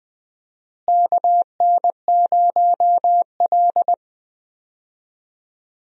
Morse code, heard as a keyed tone, about 20 wpm, 700 Hz. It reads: KN0L
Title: USA Call Signs - 4 Characters - Repeated 3x - 20wpm